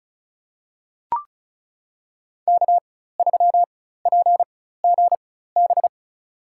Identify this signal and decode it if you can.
Morse code, heard as a keyed tone, about 36 wpm, spaced as for 16 wpm, 700 Hz. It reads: K3PGB
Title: USA Call Signs - 5 Characters - 4x Character Spacing - 35wpm